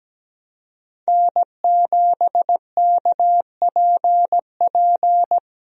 Morse code, heard as a keyed tone, about 17 wpm, 700 Hz. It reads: N7KPP
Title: USA Call Signs - 5 Characters - 17wpm